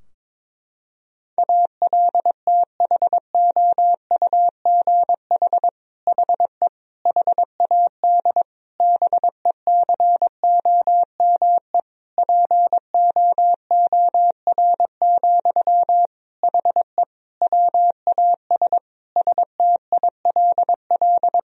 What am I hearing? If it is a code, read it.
Morse code, heard as a keyed tone, about 22 wpm, 700 Hz. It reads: ALTHOUGH HE HAD BECOME POOR, HE WAS STILL
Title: Sentences from Top 1000 Words - Review Part 3 - 22wpm